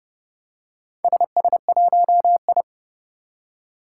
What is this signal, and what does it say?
Morse code, heard as a keyed tone, about 30 wpm, 700 Hz. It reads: SS1I
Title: Four Challenging Letter-Number Combinations 30wpm